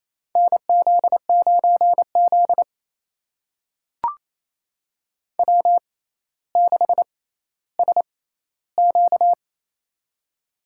Morse code, heard as a keyed tone, about 28 wpm, spaced as for 14 wpm, 700 Hz. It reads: NZ9Z W6SQ
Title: USA Call Signs - 4 Characters - 6x Character Spacing - 28wpm